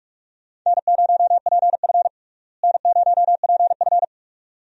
Morse code, heard as a keyed tone, about 46 wpm, 700 Hz. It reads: N0PF N0PF
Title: USA Call Signs - 4 Characters - Repeated 2x - 45wpm